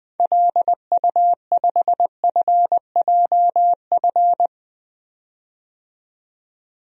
Morse code, heard as a keyed tone, about 20 wpm, 700 Hz. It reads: LU5FJF
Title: DXCC-100 Call Signs - 6 Characters - 20wpm